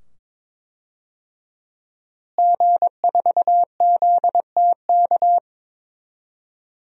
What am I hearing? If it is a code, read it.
Morse code, heard as a keyed tone, about 22 wpm, 700 Hz. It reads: G4ZTK
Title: DXCC-100 Call Signs - 5 Characters - Repeated 2x - 22wpm